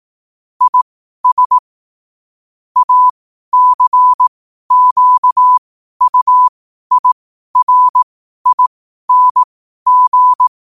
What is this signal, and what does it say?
Morse code, heard as a keyed tone, about 18 wpm, spaced as for 12 wpm, 1000 Hz. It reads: IS ACQUIRING
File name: qotdfiles / 12wpm